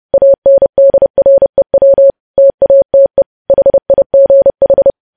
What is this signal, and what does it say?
Morse code, heard as a keyed tone, about 30 wpm, 550 Hz. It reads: ANDREW TATE HIGH
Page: news headlines in morse code at 30 wpm